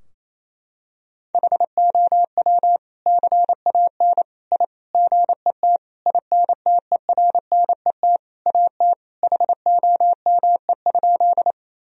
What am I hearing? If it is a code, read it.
Morse code, heard as a keyed tone, about 28 wpm, 700 Hz. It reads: HOW CAN I GET INTERNET AT HOME?
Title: Sentences from Top 1000 Words - Review Part 4 - 28wpm